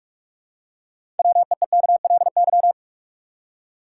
Morse code, heard as a keyed tone, about 46 wpm, 700 Hz. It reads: WEEKLY
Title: Six Letter Words - 45wpm